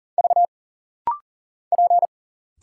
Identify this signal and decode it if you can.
Morse code, heard as a keyed tone, about 40 wpm, 700 Hz. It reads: V P
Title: Single Letter-Number Alt - Rapid-Fire 40wpm